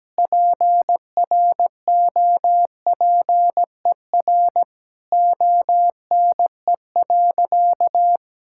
Morse code, heard as a keyed tone, about 17 wpm, 700 Hz. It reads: PROPER ONE.
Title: Sentences from Top 1000 Words - Review Part 5 - 17wpm